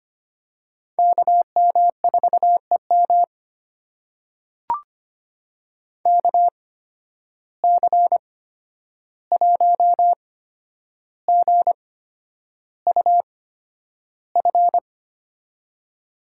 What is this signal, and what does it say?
Morse code, heard as a keyed tone, about 25 wpm, spaced as for 10 wpm, 700 Hz. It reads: KM4EM KC1GUF
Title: USA Call Signs - 8x Character Spacing - 25wpm